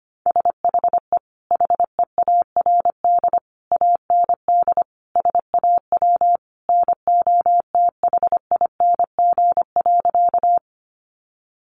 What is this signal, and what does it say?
Morse code, heard as a keyed tone, about 25 wpm, 700 Hz. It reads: SHE HEARD AND SAW NOTHING.
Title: Sentences from Top 500 Words 25wpm